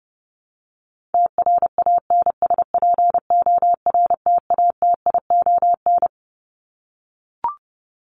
Morse code, heard as a keyed tone, about 30 wpm, 700 Hz. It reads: TRANSPORTATION
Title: Top 1501-1750 Words 30wpm